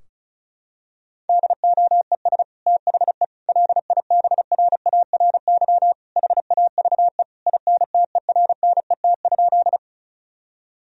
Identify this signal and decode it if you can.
Morse code, heard as a keyed tone, about 36 wpm, 700 Hz. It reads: DOES THE LIBRARY HAVE INTERNET?
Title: Sentences from Top 1000 Words - Review Part 4 - 35wpm